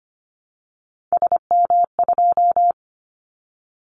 Morse code, heard as a keyed tone, about 25 wpm, 700 Hz. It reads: SM2
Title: Callsign Prefixes for Top 25 DXCC - Repeated 3x - 25wpm